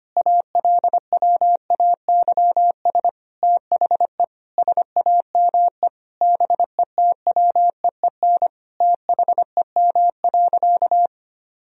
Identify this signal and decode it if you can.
Morse code, heard as a keyed tone, about 25 wpm, 700 Hz. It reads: ALWAYS THE SAME BETWEEN THEM.
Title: Sentences from Top 1000 Words - Review Part 1 - 25wpm